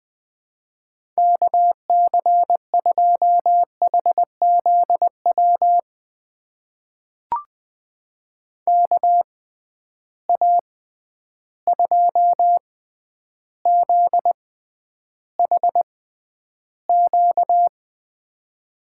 Morse code, heard as a keyed tone, about 20 wpm, spaced as for 10 wpm, 700 Hz. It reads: KC2HZW KA2ZHQ
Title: USA Call Signs - 6 Characters - 6x Character Spacing - 20wpm